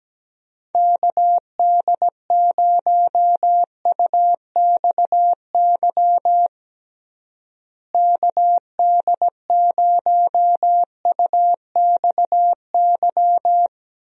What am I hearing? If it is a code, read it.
Morse code, heard as a keyed tone, about 17 wpm, 700 Hz. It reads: KD0UXY KD0UXY
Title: USA Call Signs - 6 Characters - Repeated 3x - 17wpm